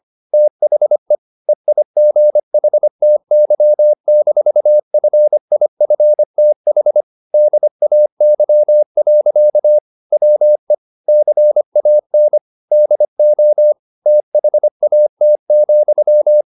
Morse code, heard as a keyed tone, about 25 wpm, 600 Hz. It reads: THE EIGHTY-FIFTH DAY. WE CAN DO THAT,
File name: the old man and the sea part 1 25wpm